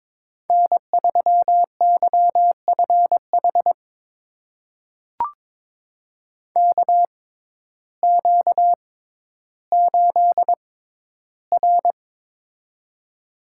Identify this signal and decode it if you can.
Morse code, heard as a keyed tone, about 22 wpm, spaced as for 12 wpm, 700 Hz. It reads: N3YFH KQ8R
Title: USA Call Signs - 6x Character Spacing - 22wpm